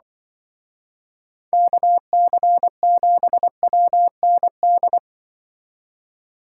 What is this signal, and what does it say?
Morse code, heard as a keyed tone, about 25 wpm, 700 Hz. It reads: KC7WND
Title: Speed Racing - USA Call Signs 20wpm